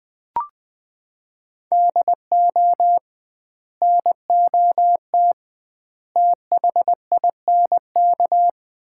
Morse code, heard as a keyed tone, about 21 wpm, 700 Hz. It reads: DO NOT THINK
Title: 2x Word-Spacing - Sets of 4 Words - 20wpm